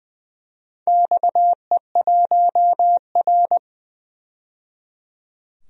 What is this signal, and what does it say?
Morse code, heard as a keyed tone, about 20 wpm, 700 Hz. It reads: XE1R
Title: DXCC-100 Call Signs - 4 Characters - 20wpm